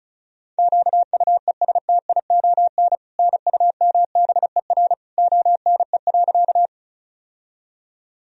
Morse code, heard as a keyed tone, about 36 wpm, 700 Hz. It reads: QUESTION NUMBER ONE.
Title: Sentences from Top 1000 Words - Review Part 1 - 35wpm